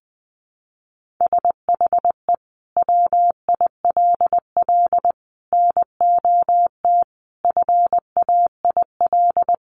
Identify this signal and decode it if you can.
Morse code, heard as a keyed tone, about 20 wpm, 700 Hz. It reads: SHE WILL NOT FAIL
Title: Sentences from Top 1100 Words 20wpm